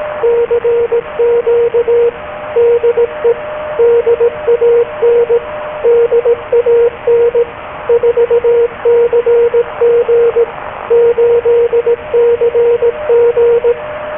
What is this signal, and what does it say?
Morse code, heard as a keyed tone, about 18 wpm, 470 Hz, with radio noise. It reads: CQ DE DAN DAN 4CG 8CG
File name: DAN-Audio / CQ-A1